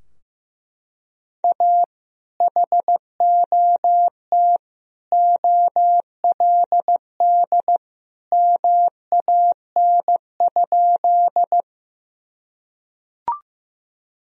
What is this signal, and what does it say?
Morse code, heard as a keyed tone, about 15 wpm, 700 Hz. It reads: A HOT OLD MAN?
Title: Sentences from Top 1000 Words - Review Part 3 - 15wpm